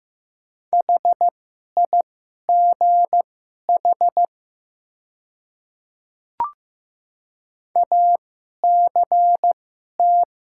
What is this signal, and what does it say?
Morse code, heard as a keyed tone, about 15 wpm, spaced as for 11 wpm, 700 Hz. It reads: HIGH ACT
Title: 2x Character Spacing - Top 200 Words - 15wpm